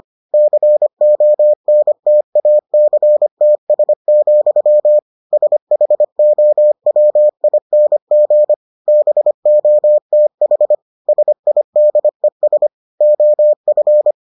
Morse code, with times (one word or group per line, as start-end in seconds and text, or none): 0.3-5.0 CONTACTS,
5.3-8.5 SHOWING
8.9-10.8 BOTH
11.1-12.7 SIDES
13.0-14.1 OF